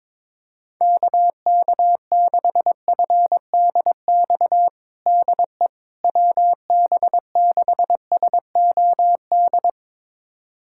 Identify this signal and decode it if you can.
Morse code, heard as a keyed tone, about 22 wpm, 700 Hz. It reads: KK6FDX DE WB6SOD
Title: USA Callsign DE Callsign - 6 Characters - 22wpm